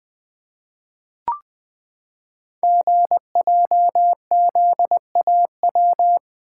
Morse code, heard as a keyed tone, about 20 wpm, 700 Hz. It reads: GJZAW
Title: Code Groups - 5 Letters - 20wpm